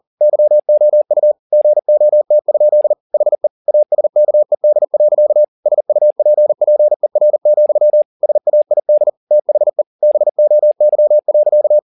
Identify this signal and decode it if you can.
Morse code, heard as a keyed tone, about 40 wpm, 600 Hz. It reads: YOU GOT? HE ASKED. SUPPER, SAID THE BOY.